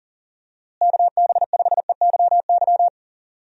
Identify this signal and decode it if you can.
Morse code, heard as a keyed tone, about 40 wpm, 700 Hz. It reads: KB5EYY
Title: USA Call Signs - 6 Characters - Repeated 2x - 40wpm